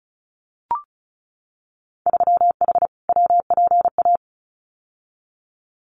Morse code, heard as a keyed tone, about 36 wpm, 700 Hz. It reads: 3H WPA
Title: Winter Field Day Exchanges 35wpm